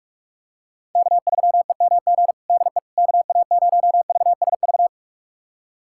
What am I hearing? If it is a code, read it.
Morse code, heard as a keyed tone, about 46 wpm, 700 Hz. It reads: K3EMG DE KA0VSV
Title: USA Callsign DE Callsign 45wpm